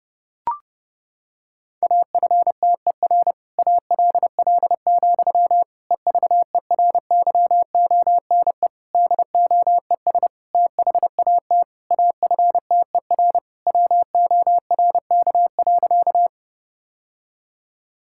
Morse code, heard as a keyed tone, about 28 wpm, 700 Hz. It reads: AFTER ALL, EVERYONE DOES THAT AFTER WORK.
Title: Sentences from Top 1000 Words - Review Part 5 - 30wpm